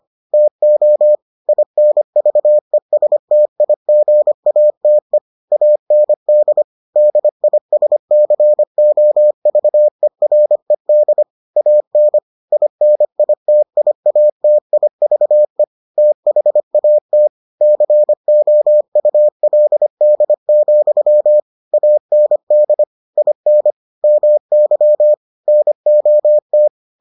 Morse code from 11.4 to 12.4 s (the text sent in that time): AN